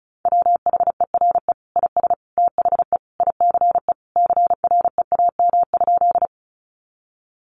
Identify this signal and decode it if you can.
Morse code, heard as a keyed tone, about 34 wpm, 700 Hz. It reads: WHERE IS THE ICE CREAM?
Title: Sentences from Top 1500 Words 35wpm